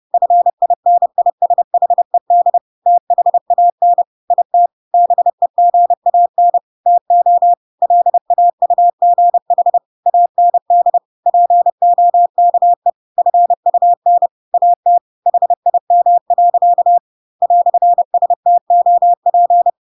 Morse code, heard as a keyed tone, about 30 wpm, 700 Hz. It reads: FINISHED THAN IT BEGAN TO LAUGH AND POKE FUN AT HIM. "STOP
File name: Pinocchio / Pinocchio ch3 30 30 ik1hge